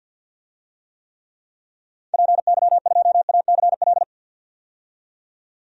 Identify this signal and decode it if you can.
Morse code, heard as a keyed tone, about 50 wpm, 700 Hz. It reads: PY2ACL
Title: DXCC-100 Call Signs - 6 Characters - 50wpm